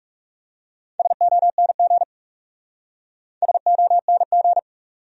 Morse code, heard as a keyed tone, about 40 wpm, 700 Hz. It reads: SONG SONG